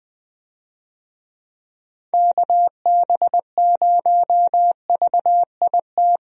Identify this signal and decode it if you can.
Morse code, heard as a keyed tone, about 20 wpm, 700 Hz. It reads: KB0VIT
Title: USA Call Signs - 6 Characters - Repeated 3x - 20wpm